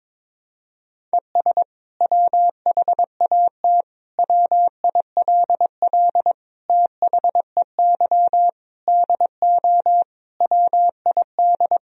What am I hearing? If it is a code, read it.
Morse code, heard as a keyed tone, about 22 wpm, 700 Hz. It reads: ES WHAT WILL THEY DO WID
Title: Sentences from Top 200 Words and Abbreviations 22wpm